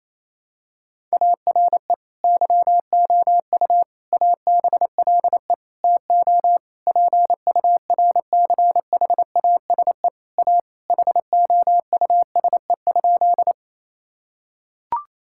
Morse code, heard as a keyed tone, about 27 wpm, 700 Hz. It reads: ARE YOU ABLE TO PURCHASE A HOUSE?
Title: Sentences from Top 1000 Words - Review Part 5 - 28wpm